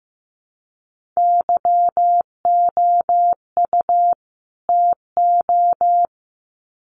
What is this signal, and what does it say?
Morse code, heard as a keyed tone, about 15 wpm, 700 Hz. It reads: YOU TO
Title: Sets of 2 Words 15wpm